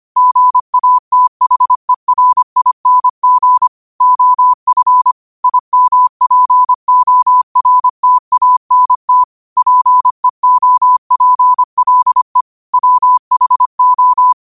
Morse code, heard as a keyed tone, about 25 wpm, 1000 Hz. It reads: GATHERING OF IMPORTANT PEOPLE WHO
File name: qotdfiles / 25wpm